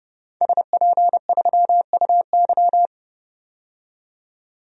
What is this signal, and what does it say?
Morse code, heard as a keyed tone, about 30 wpm, 700 Hz. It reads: SP3UY
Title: DXCC-100 Call Signs - 5 Characters - 30wpm